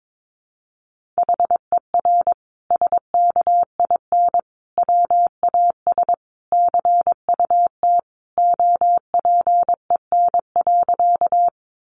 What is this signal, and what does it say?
Morse code, heard as a keyed tone, about 22 wpm, 700 Hz. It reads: HER SKIN WAS CUT OPEN.
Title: Sentences from Top 1300 Words 22wpm